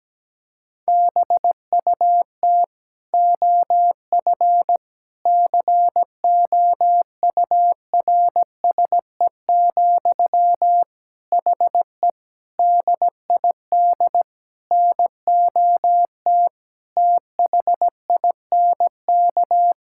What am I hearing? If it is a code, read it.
Morse code, heard as a keyed tone, about 17 wpm, 700 Hz. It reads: BUT OF COURSE, HE DID NOT THINK
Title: Sentences from Top 1600 Words 17wpm